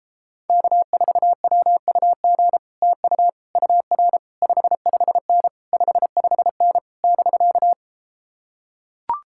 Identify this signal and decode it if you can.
Morse code, heard as a keyed tone, about 33 wpm, 700 Hz. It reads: K4WUG TU UR 55N 55N <BK>